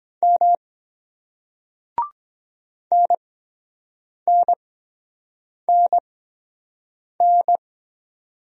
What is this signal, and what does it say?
Morse code, heard as a keyed tone, about 23 wpm, spaced as for 7 wpm, 700 Hz. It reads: M NNNN